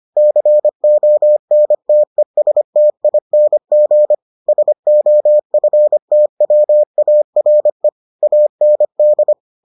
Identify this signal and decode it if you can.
Morse code, heard as a keyed tone, about 25 wpm, 600 Hz. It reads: CONTESTING SOFTWARE AND